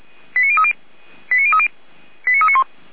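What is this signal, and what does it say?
Five-tone selective call (ZVEI), heard as quick runs of five tones, tones of about 70 ms. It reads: 89030 89030 89391
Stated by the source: Received on SDR, ZVEI five-tone call.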